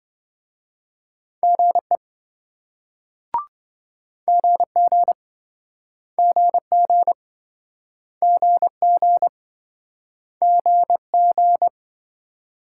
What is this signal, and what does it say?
Morse code, heard as a keyed tone, about 27 wpm, 700 Hz. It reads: GE GG GG GG GG